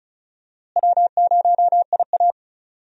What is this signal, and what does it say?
Morse code, heard as a keyed tone, about 36 wpm, 700 Hz. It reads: W0IA
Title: USA Call Signs - 4 Characters - Repeated 3x - 35wpm